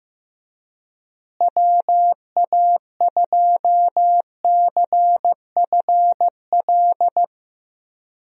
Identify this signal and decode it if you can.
Morse code, heard as a keyed tone, about 15 wpm, 700 Hz. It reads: WA2CFL
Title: USA Call Signs - 6 Characters - Repeated 3x - 15wpm